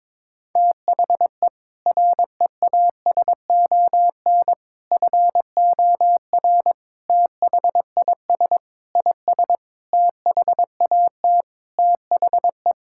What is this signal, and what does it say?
Morse code, heard as a keyed tone, about 22 wpm, 700 Hz. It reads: THE REASON FOR THIS IS THAT THE
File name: Sentences with 7-Word Phrases - 22wpm